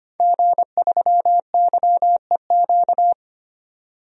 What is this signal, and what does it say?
Morse code, heard as a keyed tone, about 25 wpm, 700 Hz. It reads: G3YEQ
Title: DXCC-100 Call Signs - 5 Characters - 25wpm